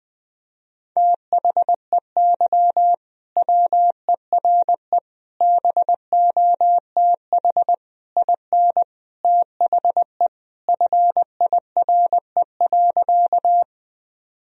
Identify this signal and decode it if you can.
Morse code, heard as a keyed tone, about 21 wpm, 700 Hz. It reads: THEY WERE BOTH IN THE FIRE.
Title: Sentences from Top 1000 Words - Review Part 1 - 20wpm